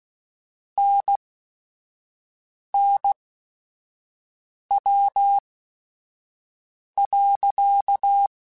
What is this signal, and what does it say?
Morse code, heard as a keyed tone, about 16 wpm, spaced as for 5 wpm, 800 Hz. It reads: NNW.
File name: Weather Sets / Weather Sets-mp3s / Weather 5 22 @5